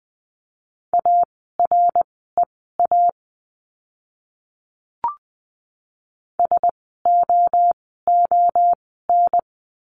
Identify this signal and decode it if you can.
Morse code, heard as a keyed tone, about 20 wpm, spaced as for 15 wpm, 700 Hz. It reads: AREA SOON
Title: 075 Lesson 13f - Letter L - Words 20wpm